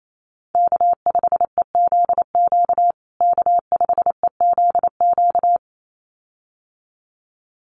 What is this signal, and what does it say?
Morse code, heard as a keyed tone, about 28 wpm, 700 Hz. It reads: K5EZQ K5EZQ
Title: USA Call Signs Repeated 2x 28wpm